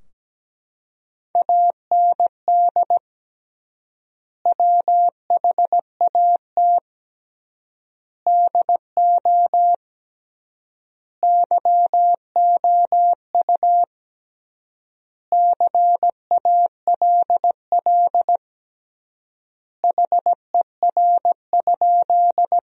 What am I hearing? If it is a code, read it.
Morse code, heard as a keyed tone, about 17 wpm, 700 Hz. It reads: AND WHAT DO YOU CALL HER?
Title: 3x Word-Spacing - Sentences from Top 100 Words - 17wpm